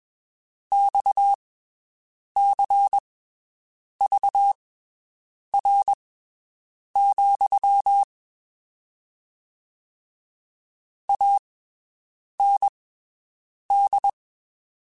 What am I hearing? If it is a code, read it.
Morse code, heard as a keyed tone, about 21 wpm, spaced as for 7 wpm, 800 Hz. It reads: XCVR, AND
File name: (spaced) / QSOs Spaced-mp3s / WD0FBT 1 @20 sp